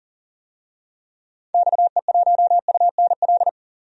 Morse code, heard as a keyed tone, about 37 wpm, 700 Hz. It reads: XE1UNL
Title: DXCC-100 Call Signs - 6 Characters - Repeated 3x - 40wpm